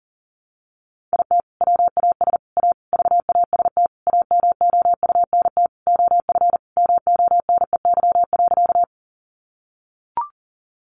Morse code, heard as a keyed tone, about 40 wpm, 700 Hz. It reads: IT WAS A VAST AMOUNT OF MONEY.
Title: Sentences from Top 1300 Words 40wpm